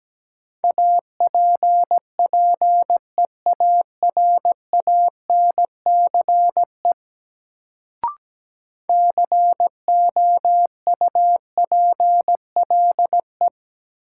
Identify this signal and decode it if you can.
Morse code, heard as a keyed tone, about 17 wpm, 700 Hz. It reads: APPEARANCE COUPLE
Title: Sentences from Top 1100 Words - New Word Drill Alt - 17wpm